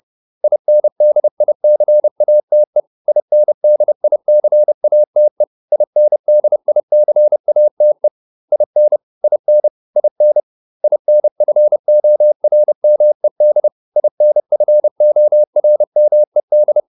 Morse code, heard as a keyed tone, about 30 wpm, 600 Hz. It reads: INDICATE INDICATE INDICATE IN IN IN INFORMED INFORMED